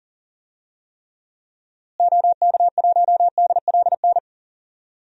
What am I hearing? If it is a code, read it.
Morse code, heard as a keyed tone, about 40 wpm, 700 Hz. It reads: OK1DLN
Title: DXCC-100 Call Signs - 6 Characters - Repeated 3x - 40wpm